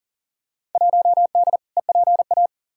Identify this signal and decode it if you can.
Morse code, heard as a keyed tone, about 40 wpm, 700 Hz. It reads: 1D EPA
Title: Field Day Exchanges - 40wpm